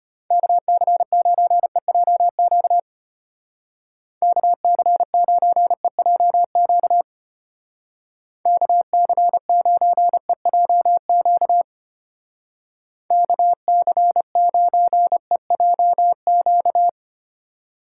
Morse code, heard as a keyed tone, about 33 wpm, 700 Hz. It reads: KC9EJQ KC9EJQ KC9EJQ KC9EJQ